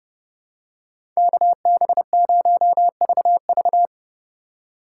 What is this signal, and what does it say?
Morse code, heard as a keyed tone, about 31 wpm, 700 Hz. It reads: KB0VV
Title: USA Call Signs - 4x Character Spacing - 30wpm